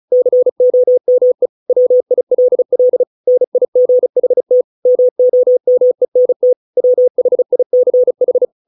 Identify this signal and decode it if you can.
Morse code, heard as a keyed tone, about 34 wpm, 500 Hz. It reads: COME WILL NIGHT MOMENT WHICH
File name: 632 words / 632 Words 35WPM